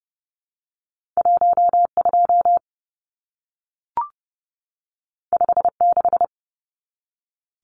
Morse code, heard as a keyed tone, about 30 wpm, 700 Hz. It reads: 12 56
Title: Numbers - 2-Digit 30wpm